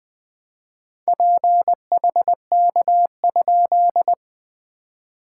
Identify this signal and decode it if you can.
Morse code, heard as a keyed tone, about 21 wpm, 690 Hz. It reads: PHK?
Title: SOTA QSO Practice 20wpm